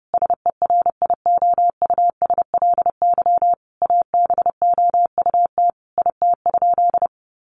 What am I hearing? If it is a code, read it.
Morse code, heard as a keyed tone, about 30 wpm, 700 Hz. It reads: SERIOUSLY ABOUT IT?
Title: Sentences from Top 1200 Words 30wpm